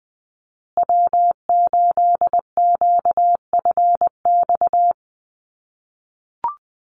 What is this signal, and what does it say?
Morse code, heard as a keyed tone, about 20 wpm, 700 Hz. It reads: W8QFX
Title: USA Call Signs Repeated 2x 20wpm